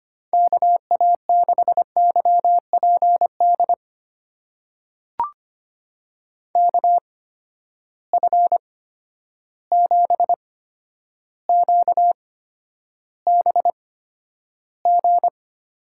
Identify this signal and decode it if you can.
Morse code, heard as a keyed tone, about 25 wpm, spaced as for 11 wpm, 700 Hz. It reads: KA6YPD KF7QBG
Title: USA Call Signs - 6 Characters - 8x Character Spacing - 25wpm